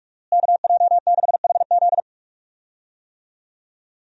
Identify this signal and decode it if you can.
Morse code, heard as a keyed tone, about 41 wpm, 700 Hz. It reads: KJ6HZ
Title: USA Call Signs - 5 Characters - Repeated 2x - 45wpm